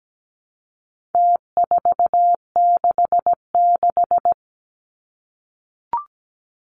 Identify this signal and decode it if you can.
Morse code, heard as a keyed tone, about 17 wpm, 700 Hz. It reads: T466
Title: Numbers - Contest Style - 4-Digit 17wpm